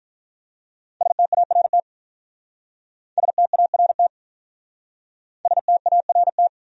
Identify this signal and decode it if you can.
Morse code, heard as a keyed tone, about 47 wpm, 700 Hz. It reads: START START START